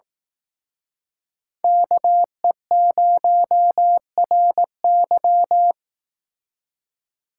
Morse code, heard as a keyed tone, about 18 wpm, 700 Hz. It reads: KE0RY
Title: Speed Racing - USA Call Signs 15wpm